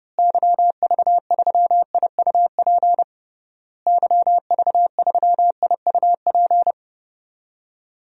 Morse code, heard as a keyed tone, about 30 wpm, 700 Hz. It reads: YV3IUP YV3IUP
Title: DXCC-100 Call Signs - 6 Characters - Repeated 3x - 30wpm